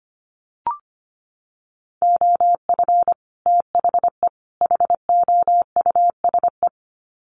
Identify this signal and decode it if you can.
Morse code, heard as a keyed tone, about 25 wpm, 700 Hz. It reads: OF THE HOUSE
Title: Sets of 3 Words 25wpm